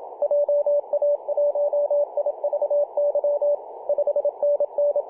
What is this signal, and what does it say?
Morse code, heard as a keyed tone, about 28 wpm, 580 Hz, with radio noise. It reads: JA1IVY 5NN
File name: PJ2/G3TXF 18.086MHz CW